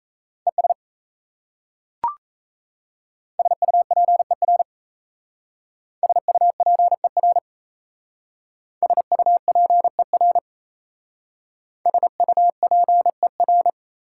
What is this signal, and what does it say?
Morse code, heard as a keyed tone, about 36 wpm, 700 Hz. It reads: ES SUPER SUPER SUPER SUPER